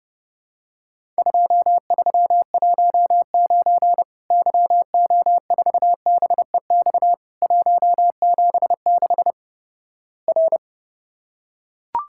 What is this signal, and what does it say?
Morse code, heard as a keyed tone, about 30 wpm, 700 Hz. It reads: 2319 YO4BEX 176 R
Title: Worked All Europe DX Contest - QTC Exchange - 30wpm